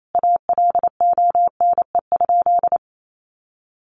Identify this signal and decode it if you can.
Morse code, heard as a keyed tone, about 28 wpm, 700 Hz. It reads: ALONE?
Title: Sentences from Top 1000 Words 28wpm